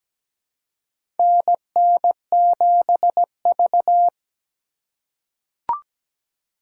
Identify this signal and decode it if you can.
Morse code, heard as a keyed tone, about 17 wpm, 700 Hz. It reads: NN7V
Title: USA Call Signs - 4x Character Spacing - 17wpm